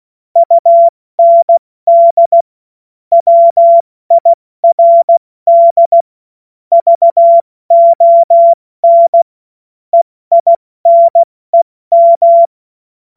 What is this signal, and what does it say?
Morse code, heard as a keyed tone, about 16 wpm, spaced as for 14 wpm, 680 Hz. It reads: UND WIRD VON EINEM
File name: Morsekurs / Witze / Witz 28. Dezember 2021, 70 F80